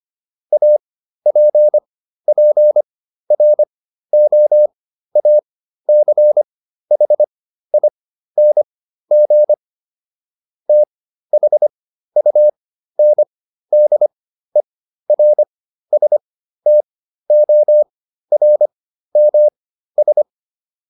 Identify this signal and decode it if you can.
Morse code, heard as a keyed tone, about 25 wpm, spaced as for 13 wpm, 600 Hz. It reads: APPROACHING THUNDERSTORMS